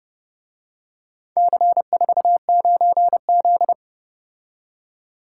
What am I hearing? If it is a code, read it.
Morse code, heard as a keyed tone, about 29 wpm, 700 Hz. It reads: C49Z